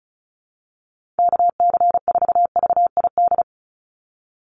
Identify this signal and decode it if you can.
Morse code, heard as a keyed tone, about 34 wpm, 700 Hz. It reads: KC4VID
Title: USA Call Signs 35wpm